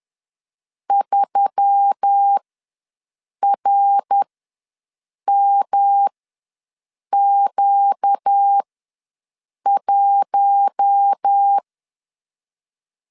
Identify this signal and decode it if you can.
Morse code, heard as a keyed tone, about 10 wpm, spaced as for 6 wpm, 800 Hz. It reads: 3RMQ1
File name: LEZIONI / 5 LBVC / 5 LBVC ws 10wpm 3733